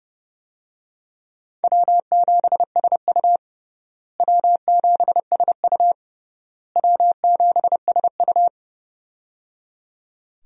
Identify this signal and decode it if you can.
Morse code, heard as a keyed tone, about 30 wpm, 700 Hz. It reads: W7SU W7SU W7SU
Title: USA Call Signs - 4 Characters - Repeated 3x - 30wpm